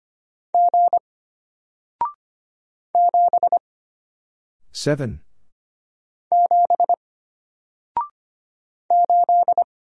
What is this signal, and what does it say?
Morse code, heard as a keyed tone, about 25 wpm, 700 Hz. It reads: G 7 7 8
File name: Single-Letter-Number-25wpm